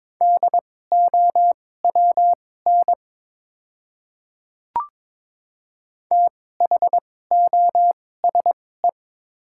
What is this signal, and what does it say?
Morse code, heard as a keyed tone, about 22 wpm, spaced as for 16 wpm, 690 Hz. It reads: DOWN THOSE
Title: 2x Character Spacing - Top 300 Words - 22wpm